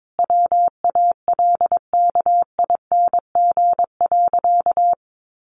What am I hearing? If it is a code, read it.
Morse code, heard as a keyed tone, about 22 wpm, 700 Hz. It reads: WALKING.
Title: Easy Phrases 22wpm